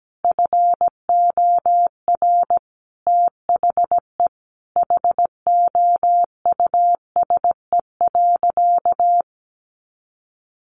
Morse code, heard as a keyed tone, about 17 wpm, 700 Hz. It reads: FOR THE HOUSE.
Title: Sentences from Top 1400 Words 17wpm